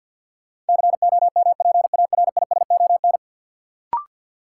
Morse code, heard as a keyed tone, about 53 wpm, 700 Hz. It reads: COMPARISON
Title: Sentences from Top 1600 Words - New Word Drill Alt - 50wpm